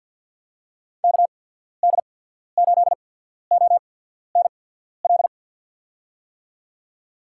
Morse code, heard as a keyed tone, about 53 wpm, spaced as for 14 wpm, 700 Hz. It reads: KD8ONL
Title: USA Call Signs - 6 Characters - 8x Character Spacing - 50wpm